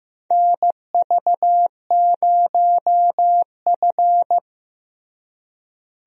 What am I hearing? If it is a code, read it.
Morse code, heard as a keyed tone, about 15 wpm, 700 Hz. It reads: NV0F
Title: USA Call Signs - 4 Characters - Repeated 3x - 15wpm